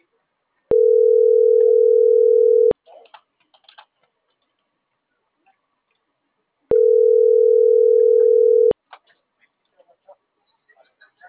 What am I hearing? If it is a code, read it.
Ringback tone (North America), 440 + 480 Hz.